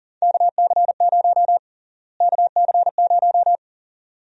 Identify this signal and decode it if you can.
Morse code, heard as a keyed tone, about 40 wpm, 700 Hz. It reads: KC0 KC0
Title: USA Call Sign Prefixes - Repeated 2x - 40wpm